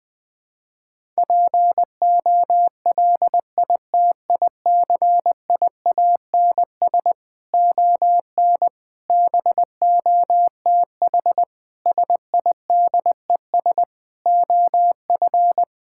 Morse code, heard as a keyed tone, about 20 wpm, 700 Hz. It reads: POLITICIANS ON BOTH SIDES OF
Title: Sentences with 6-Word Phrases - 20wpm